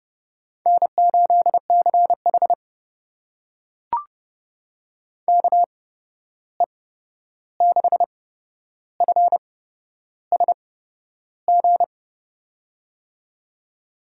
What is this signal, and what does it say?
Morse code, heard as a keyed tone, about 30 wpm, spaced as for 11 wpm, 700 Hz. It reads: N8CH KE6FSG
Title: USA Call Signs - 8x Character Spacing - 30wpm